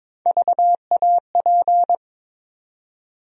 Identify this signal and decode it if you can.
Morse code, heard as a keyed tone, about 22 wpm, 700 Hz. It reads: VAP
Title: USA Call Sign Postfixes - 3 Letters 22wpm